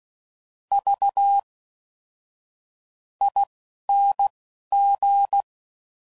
Morse code, heard as a keyed tone, about 16 wpm, spaced as for 7 wpm, 800 Hz. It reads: VING